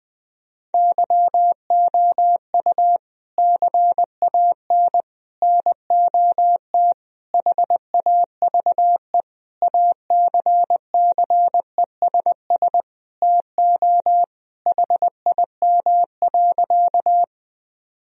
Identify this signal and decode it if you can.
Morse code, heard as a keyed tone, about 20 wpm, 700 Hz. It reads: YOU CAN NOT HAVE ACCESS TO HIM.
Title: Sentences from Top 1000 Words - Review Part 4 - 20wpm